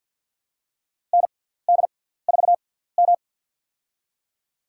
Morse code, heard as a keyed tone, about 53 wpm, spaced as for 17 wpm, 700 Hz. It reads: ND4M
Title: USA Call Signs - 4 Characters - 6x Character Spacing - 50wpm